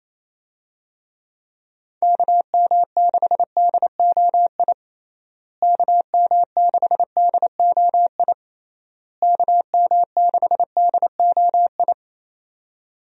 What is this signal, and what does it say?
Morse code, heard as a keyed tone, about 28 wpm, 700 Hz. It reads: KM6DOI KM6DOI KM6DOI
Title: USA Call Signs - 6 Characters - Repeated 3x - 28wpm